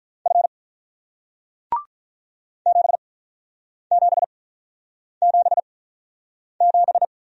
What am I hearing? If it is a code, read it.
Morse code, heard as a keyed tone, about 45 wpm, spaced as for 9 wpm, 700 Hz. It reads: F 7777